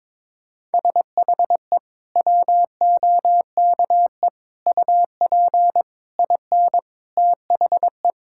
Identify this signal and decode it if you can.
Morse code, heard as a keyed tone, about 22 wpm, 700 Hz. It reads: SHE WOKE UP IN THE
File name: practice / 22wpm / Sentences with 8-Word Phrases - 22wpm